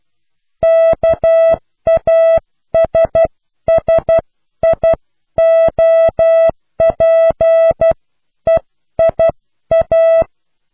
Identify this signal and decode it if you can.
Morse code, heard as a keyed tone, about 12 wpm, spaced as for 10 wpm, 660 Hz. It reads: KASSIOPEIA
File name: cw / pseudo worden(10)